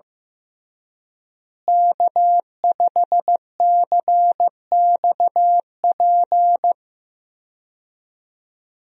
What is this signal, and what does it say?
Morse code, heard as a keyed tone, about 15 wpm, 700 Hz. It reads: K5CXP
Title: USA Call Signs - 5 Characters - Repeated 3x - 15wpm